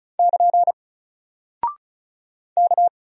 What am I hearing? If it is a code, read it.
Morse code, heard as a keyed tone, about 35 wpm, 700 Hz. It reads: <KN> K
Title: Single Character (Letter:Number:Punctuation:Prosign) 35wpm